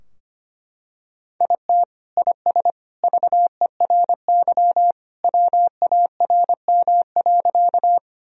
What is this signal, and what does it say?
Morse code, heard as a keyed tone, about 25 wpm, 700 Hz. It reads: IT IS VERY WARM.
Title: Sentences from Top 1000 Words - Review Part 2 - 25wpm